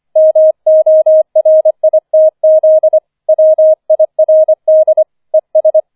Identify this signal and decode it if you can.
Morse code, heard as a keyed tone, about 25 wpm, 630 Hz. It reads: MORITZ WIRD ES